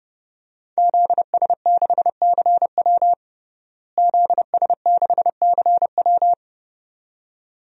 Morse code, heard as a keyed tone, about 29 wpm, 700 Hz. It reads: ZS6CW ZS6CW
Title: DXCC-100 Call Signs - 5 Characters - Repeated 2x - 30wpm